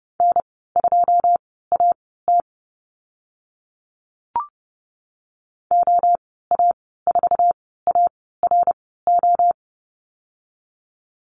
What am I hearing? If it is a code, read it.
Morse code, heard as a keyed tone, about 30 wpm, spaced as for 17 wpm, 700 Hz. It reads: N2AT OA4ARO
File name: Lesson 14g - Number 2 - Call Signs 30wpm